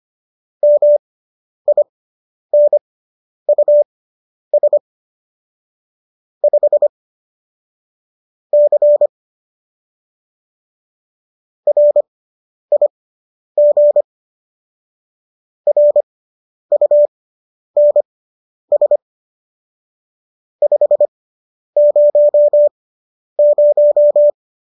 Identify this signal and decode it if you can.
Morse code, heard as a keyed tone, about 25 wpm, spaced as for 10 wpm, 590 Hz. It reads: MINUS 5 C RIG RUNS 500
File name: qso219 10